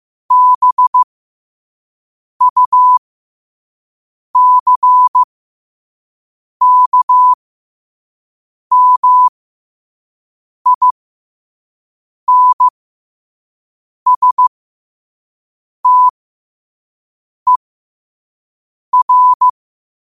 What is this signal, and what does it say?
Morse code, heard as a keyed tone, about 15 wpm, spaced as for 5 wpm, 1000 Hz. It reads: BUCKMINSTER
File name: qotdfiles / 5wpm